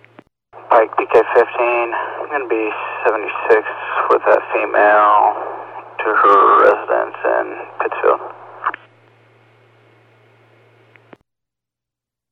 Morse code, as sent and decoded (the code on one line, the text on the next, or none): none